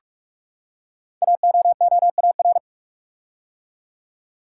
Morse code, heard as a keyed tone, about 44 wpm, 700 Hz. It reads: AOOAR
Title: Code Groups - 5 Letter:Numbers - 45wpm